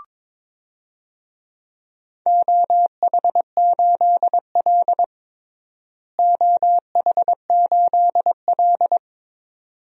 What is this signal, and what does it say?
Morse code, heard as a keyed tone, about 22 wpm, 700 Hz. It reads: OH8L OH8L
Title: DXCC-100 Call Signs - 4 Characters - Repeated 2x - 22wpm